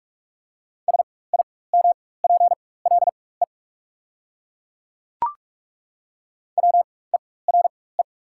Morse code, heard as a keyed tone, about 42 wpm, spaced as for 21 wpm, 690 Hz. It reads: SIMPLE WERE
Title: 4x Character Spacing - Top 500 Words - 45wpm